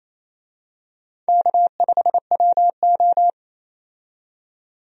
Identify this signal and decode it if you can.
Morse code, heard as a keyed tone, about 28 wpm, 700 Hz. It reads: K5WO
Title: USA Call Signs - 4 Characters - 28wpm